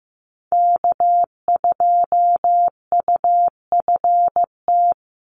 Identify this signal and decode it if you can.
Morse code, heard as a keyed tone, about 15 wpm, 700 Hz. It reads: K2UFT
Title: Call Signs Repeated 3x - 15wpm